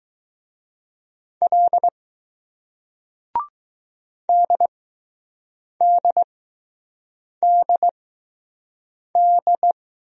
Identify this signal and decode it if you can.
Morse code, heard as a keyed tone, about 20 wpm, spaced as for 6 wpm, 700 Hz. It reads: L DDDD